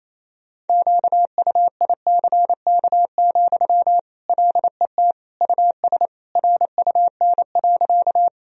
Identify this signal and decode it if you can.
Morse code, heard as a keyed tone, about 28 wpm, 700 Hz. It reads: QUICK, LET US RUN.
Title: Sentences from Top 1000 Words - Review Part 1 - 28wpm